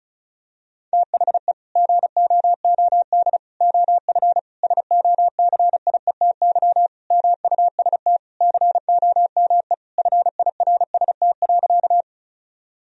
Morse code, heard as a keyed tone, about 36 wpm, 700 Hz. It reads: THE GOOD OF SOCIETY MUST COME FIRST.
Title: Sentences from Top 1000 Words - Review Part 4 - 35wpm